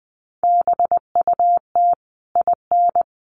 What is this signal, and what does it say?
Morse code, heard as a keyed tone, about 20 wpm, 700 Hz. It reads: BUT IN
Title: Sentences from Top 1500 Words 20wpm